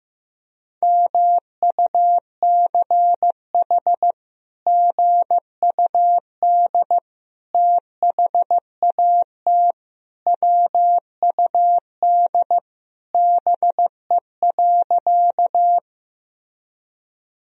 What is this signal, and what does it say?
Morse code, heard as a keyed tone, about 15 wpm, 700 Hz. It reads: MUCH GUD THAT WUD BE.
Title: Sentences from Top 200 Words and Abbreviations 15wpm